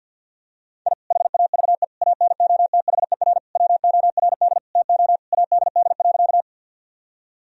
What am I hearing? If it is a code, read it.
Morse code, heard as a keyed tone, about 52 wpm, 700 Hz. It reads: I HAVE ANOTHER WORD TO ADD.
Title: Sentences from Top 1000 Words - Review Part 1 - 50wpm